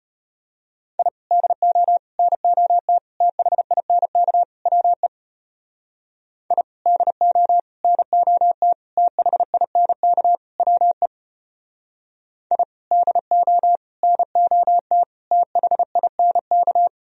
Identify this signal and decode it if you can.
Morse code, heard as a keyed tone, about 32 wpm, 700 Hz. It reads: I DO NOT THINK WE I DO NOT THINK WE I DO NOT THINK